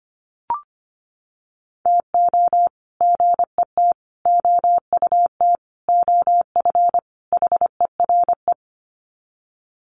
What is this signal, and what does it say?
Morse code, heard as a keyed tone, about 25 wpm, 700 Hz. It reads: TO GET OUT OF HERE
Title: Sets of 5 Words Encore 25wpm